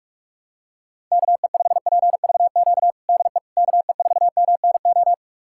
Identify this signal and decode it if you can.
Morse code, heard as a keyed tone, about 42 wpm, 700 Hz. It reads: KE5PVQ DE KE4MNO
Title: USA Callsign DE Callsign 45wpm